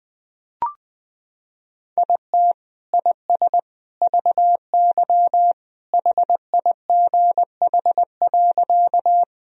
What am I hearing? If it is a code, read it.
Morse code, heard as a keyed tone, about 21 wpm, 700 Hz. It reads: IT IS VY HIGH.
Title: Sentences from Top 200 Words and Abbreviations 20wpm